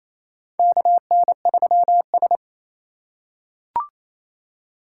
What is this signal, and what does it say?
Morse code, heard as a keyed tone, about 28 wpm, 700 Hz. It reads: KN3S
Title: USA Call Signs - 4 Characters - 2x Character Spacing - 28wpm